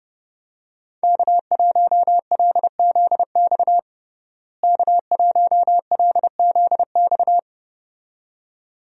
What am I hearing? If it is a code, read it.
Morse code, heard as a keyed tone, about 30 wpm, 700 Hz. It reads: K1LZX K1LZX
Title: USA Call Signs - 5 Characters - Repeated 2x - 30wpm